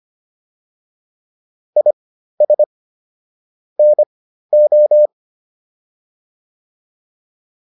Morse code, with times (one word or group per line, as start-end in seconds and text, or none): 1.8-2.6 IS
3.8-5.1 NO